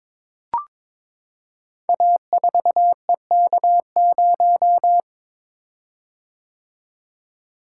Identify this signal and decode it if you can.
Morse code, heard as a keyed tone, about 22 wpm, 700 Hz. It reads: A4EK0
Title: Code Groups - 5 Letter:Numbers - 22wpm